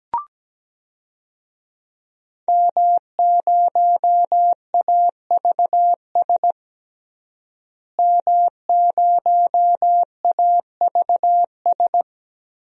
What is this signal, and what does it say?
Morse code, heard as a keyed tone, about 17 wpm, 700 Hz. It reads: M0AVS M0AVS
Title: DXCC-100 Call Signs - 5 Characters - Repeated 3x - 17wpm